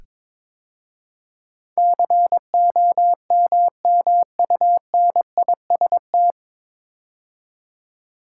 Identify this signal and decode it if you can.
Morse code, heard as a keyed tone, about 22 wpm, 700 Hz. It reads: COMMUNIST
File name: 9 Letter Words - First 3 Letters Given - 22wpm